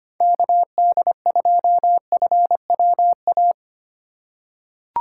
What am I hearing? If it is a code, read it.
Morse code, heard as a keyed tone, about 25 wpm, 700 Hz. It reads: KD2FWA
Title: USA Call Signs - 4x Character Spacing - 25wpm